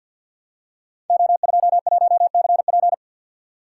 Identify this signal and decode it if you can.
Morse code, heard as a keyed tone, about 45 wpm, 700 Hz. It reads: O21CP
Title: Code Groups - 5 Letter:Numbers - 50wpm